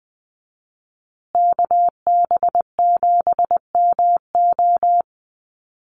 Morse code, heard as a keyed tone, about 20 wpm, 700 Hz. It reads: KB7MO
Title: USA Call Signs Repeated 2x 20wpm